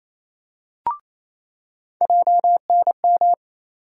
Morse code, heard as a keyed tone, about 28 wpm, 700 Hz. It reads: JNM
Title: USA Call Sign Postfixes - 3 Letters 28wpm